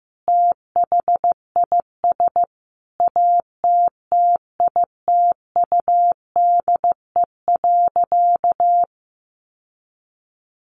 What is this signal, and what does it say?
Morse code, heard as a keyed tone, about 15 wpm, 700 Hz. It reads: THIS ATTITUDE.
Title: Sentences from Top 1300 Words 15wpm